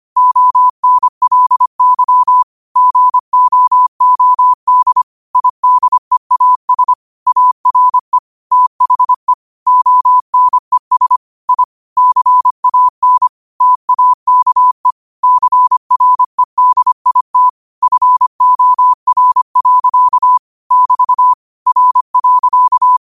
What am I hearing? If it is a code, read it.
Morse code, heard as a keyed tone, about 25 wpm, 1000 Hz. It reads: ONLY GOOD IDEAS ARE THE ONES I CAN TAKE CREDIT FOR. <BT> R.